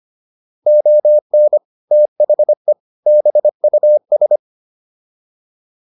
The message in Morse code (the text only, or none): ON THE BUS